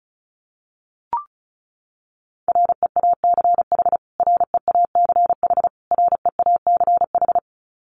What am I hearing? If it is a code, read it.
Morse code, heard as a keyed tone, about 34 wpm, 700 Hz. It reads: REACH REACH REACH
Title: Top 500 Words Repeated 35wpm